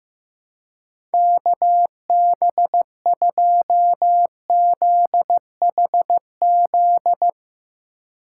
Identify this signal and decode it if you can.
Morse code, heard as a keyed tone, about 15 wpm, 700 Hz. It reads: KB2ZHZ
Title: Speed Racing - USA Call Signs 15wpm